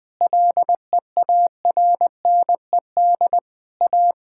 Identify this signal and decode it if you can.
Morse code, heard as a keyed tone, about 21 wpm, 700 Hz. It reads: LEARNED A
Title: Sentences with 5-Word Phrases - 20wpm